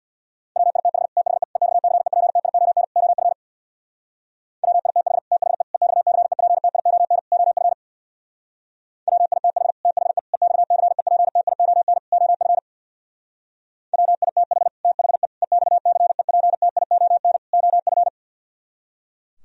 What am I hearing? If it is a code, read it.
Morse code, heard as a keyed tone, about 60 wpm, 700 Hz. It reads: WITH THE EXCEPTION OF WITH THE EXCEPTION OF WITH THE EXCEPTION OF WITH THE EXCEPTION OF
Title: Speed-Racing - Sets of 4 Words 50wpm